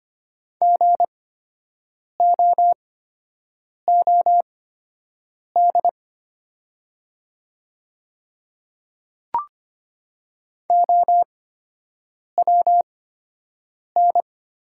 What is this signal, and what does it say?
Morse code, heard as a keyed tone, about 25 wpm, spaced as for 7 wpm, 700 Hz. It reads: GOOD OWN
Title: 8x Character Spacing - Top 300 Words - 25wpm